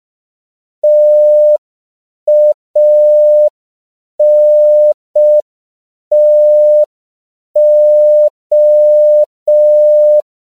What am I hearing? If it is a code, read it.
Morse code, heard as a keyed tone, about 5 wpm, 600 Hz. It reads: TANTO
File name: CW / 5PPM